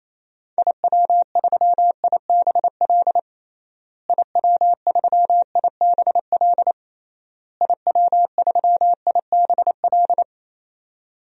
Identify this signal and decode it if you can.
Morse code, heard as a keyed tone, about 28 wpm, 700 Hz. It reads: IW3IBL IW3IBL IW3IBL
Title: DXCC-100 Call Signs - 6 Characters - Repeated 3x - 28wpm